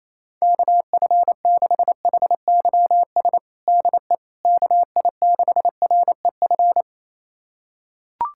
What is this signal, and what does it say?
Morse code, heard as a keyed tone, about 28 wpm, 700 Hz. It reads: KF6HYS DE KI6REF
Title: USA Callsign DE Callsign - 6 Characters - 28wpm